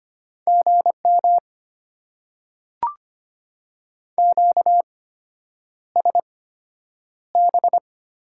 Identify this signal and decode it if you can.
Morse code, heard as a keyed tone, about 25 wpm, spaced as for 9 wpm, 700 Hz. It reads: GM QSB
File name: Top 90 QSO Elements - 8x Character Spacing - 25wpm